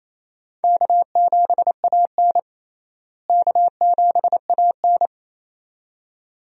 Morse code, heard as a keyed tone, about 28 wpm, 700 Hz. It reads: K7AN K7AN
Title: USA Call Signs - 4 Characters - Repeated 2x - 28wpm